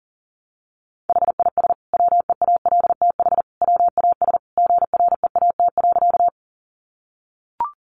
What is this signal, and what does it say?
Morse code, heard as a keyed tone, about 40 wpm, 700 Hz. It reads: HIS WEALTH WAS GREAT.
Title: Sentences from Top 1200 Words 40wpm